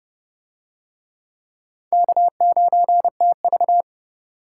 Morse code, heard as a keyed tone, about 26 wpm, 700 Hz. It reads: K9TV